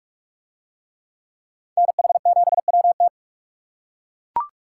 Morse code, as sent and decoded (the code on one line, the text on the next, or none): -. .... --... .-- -
NH7WT